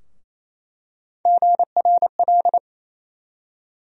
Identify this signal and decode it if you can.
Morse code, heard as a keyed tone, about 28 wpm, 700 Hz. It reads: GRL
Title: USA Call Sign Postfixes - 3 Letters 28wpm